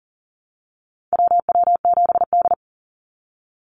Morse code, heard as a keyed tone, about 42 wpm, 700 Hz. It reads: WW7D
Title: NA SOTA Activator Callsigns 40wpm